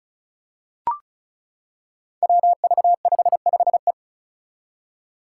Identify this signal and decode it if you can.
Morse code, heard as a keyed tone, about 34 wpm, 700 Hz. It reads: WV55E
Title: Five Challenging Letter-Number Combinations 35wpm